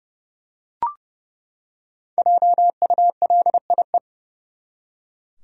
Morse code, heard as a keyed tone, about 30 wpm, 700 Hz. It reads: JULIE
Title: USA Common Names - 30wpm